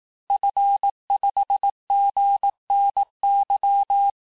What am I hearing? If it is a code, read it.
Morse code, heard as a keyed tone, about 18 wpm, 800 Hz. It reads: F5GNY
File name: cours-du-lundi files / cours-1594--FIN-de-QSO-vitesse-17mpm